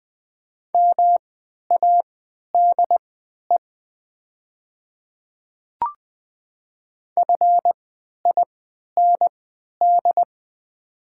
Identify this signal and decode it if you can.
Morse code, heard as a keyed tone, about 20 wpm, spaced as for 11 wpm, 690 Hz. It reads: MADE FIND
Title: 3x Character Spacing - Top 200 Words - 20wpm